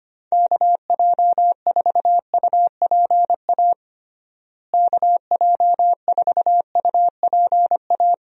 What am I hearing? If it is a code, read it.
Morse code, heard as a keyed tone, about 25 wpm, 700 Hz. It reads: KJ4UPA KJ4UPA
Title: USA Call Signs - 6 Characters - Repeated 2x - 25wpm